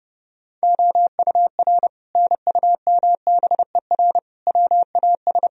Morse code, heard as a keyed tone, about 30 wpm, 700 Hz. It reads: OUR NUMBER WAS
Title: Sentences from Top 1000 Words - Review Part 2 - 30wpm